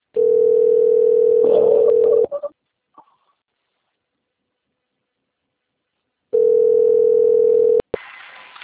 Ringback tone (North America), 440 + 480 Hz.
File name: rg-1600-unknown- - - .52747